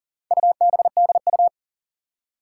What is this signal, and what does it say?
Morse code, heard as a keyed tone, about 40 wpm, 690 Hz. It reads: UBDU
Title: Four Challenging Letter-Number Combinations 40wpm